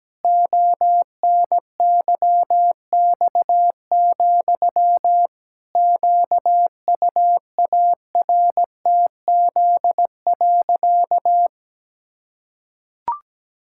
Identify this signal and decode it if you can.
Morse code, heard as a keyed tone, about 17 wpm, 700 Hz. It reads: ONYX, QUARTZ.
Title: Pangram Challenge v2 - 17wpm